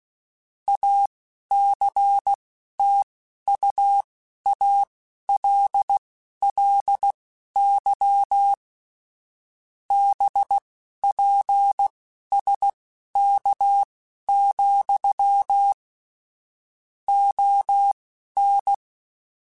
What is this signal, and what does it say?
Morse code, heard as a keyed tone, about 16 wpm, spaced as for 11 wpm, 800 Hz. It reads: ACTUALLY BPSK, ON